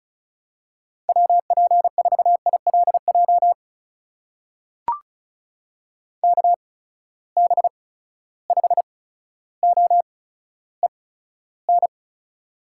Morse code, heard as a keyed tone, about 36 wpm, spaced as for 15 wpm, 700 Hz. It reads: WP4ILJ KB5OEN
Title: USA Call Signs - 8x Character Spacing - 35wpm